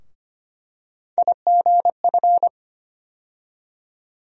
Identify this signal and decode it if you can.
Morse code, heard as a keyed tone, about 25 wpm, 700 Hz. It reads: IGF
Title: USA Call Sign Postfixes - 3 Letters 25wpm